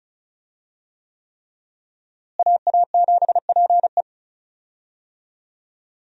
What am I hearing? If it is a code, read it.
Morse code, heard as a keyed tone, about 34 wpm, 700 Hz. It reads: AA7PE
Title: USA Call Signs - 5 Characters - Repeated 3x - 35wpm